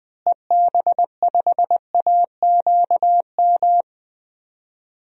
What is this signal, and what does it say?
Morse code, heard as a keyed tone, about 21 wpm, 700 Hz. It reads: EB5AQM
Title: DXCC-100 Call Signs - 6 Characters - Repeated 3x - 20wpm